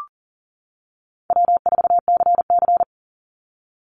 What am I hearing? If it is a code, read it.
Morse code, heard as a keyed tone, about 40 wpm, 700 Hz. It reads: W4CC
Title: Call Signs - 40wpm